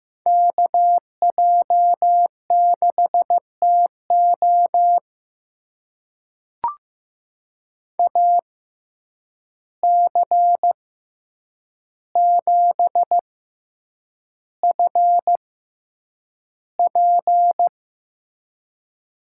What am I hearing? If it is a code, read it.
Morse code, heard as a keyed tone, about 15 wpm, spaced as for 8 wpm, 700 Hz. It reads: KJ6TO AC7FP
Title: USA Call Signs - 5 Characters - 6x Character Spacing - 15wpm